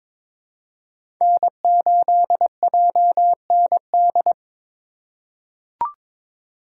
Morse code, heard as a keyed tone, about 22 wpm, 700 Hz. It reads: N8JND
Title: USA Call Signs - 5 Characters - 3x Character Spacing - 22wpm